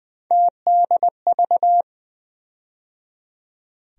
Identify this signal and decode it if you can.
Morse code, heard as a keyed tone, about 20 wpm, 700 Hz. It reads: TDV
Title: USA Call Sign Postfixes - 3 Letters 20wpm